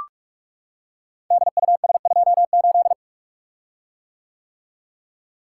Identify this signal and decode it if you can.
Morse code, heard as a keyed tone, about 42 wpm, 700 Hz. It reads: DUS28
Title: Code Groups - 5 Letter:Numbers - 45wpm